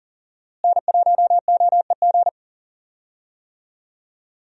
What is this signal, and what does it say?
Morse code, heard as a keyed tone, about 40 wpm, 700 Hz. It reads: N1OEG
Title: USA Call Signs - 5 Characters - Repeated 2x - 40wpm